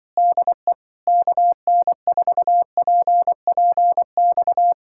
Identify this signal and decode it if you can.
Morse code, heard as a keyed tone, about 23 wpm, 700 Hz. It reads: DE KN4PPX
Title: Speed-Racing Common QSO Phrases Encore - 20wpm